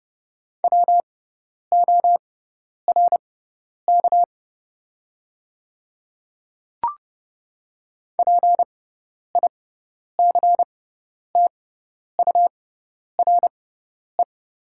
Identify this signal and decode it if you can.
Morse code, heard as a keyed tone, about 30 wpm, spaced as for 10 wpm, 700 Hz. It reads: WORK PICTURE
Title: 6x Character Spacing - Top 200 Words - 30wpm